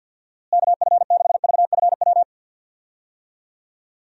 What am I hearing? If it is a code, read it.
Morse code, heard as a keyed tone, about 50 wpm, 700 Hz. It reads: KF6VFW
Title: USA Call Signs - 6 Characters - Repeated 3x - 50wpm